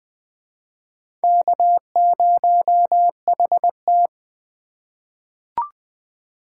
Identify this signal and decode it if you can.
Morse code, heard as a keyed tone, about 20 wpm, 700 Hz. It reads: K0HT
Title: K1USN SST QSO Practice - 20wpm